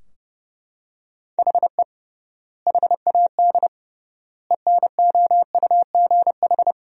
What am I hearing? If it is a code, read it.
Morse code, heard as a keyed tone, about 30 wpm, 700 Hz. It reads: HE HAD ENOUGH
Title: 3x Word-Spacing - Sentences from Top 400 Words - 30wpm